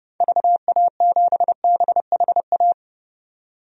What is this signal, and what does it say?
Morse code, heard as a keyed tone, about 30 wpm, 700 Hz. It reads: VA7BHA
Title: DXCC-100 Call Signs - 6 Characters - 30wpm